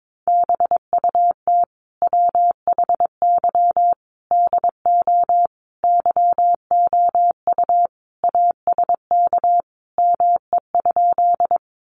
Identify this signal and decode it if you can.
Morse code, heard as a keyed tone, about 22 wpm, 700 Hz. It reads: BUT WHY DO YOU ASK ME?
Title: Sentences from Top 200 Words 22wpm